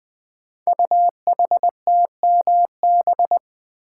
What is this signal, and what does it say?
Morse code, heard as a keyed tone, about 20 wpm, 700 Hz. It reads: UHTMB
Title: Code Groups - 5 Letter:Numbers - 20wpm